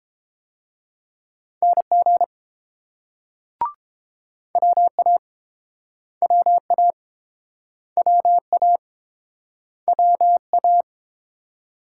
Morse code, heard as a keyed tone, about 31 wpm, 700 Hz. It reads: NG WA WA WA WA